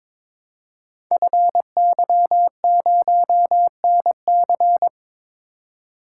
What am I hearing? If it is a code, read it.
Morse code, heard as a keyed tone, about 22 wpm, 700 Hz. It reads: FY0NC
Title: Code Groups - 5 Letter:Numbers - 22wpm